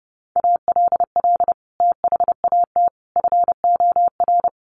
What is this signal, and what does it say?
Morse code, heard as a keyed tone, about 30 wpm, 700 Hz. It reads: ALL THAT FOR
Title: Sentences from Top 500 Words 30wpm